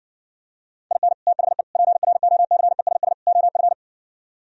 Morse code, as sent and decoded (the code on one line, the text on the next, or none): .. -. / - .... / .--. .-. --- -.-. ... ... / --- ..-.
IN THE PROCESS OF